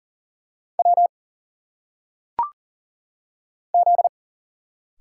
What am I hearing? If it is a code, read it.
Morse code, heard as a keyed tone, about 40 wpm, 700 Hz. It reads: W Z
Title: Single Letters - Phonetic Pronunciation 40wpm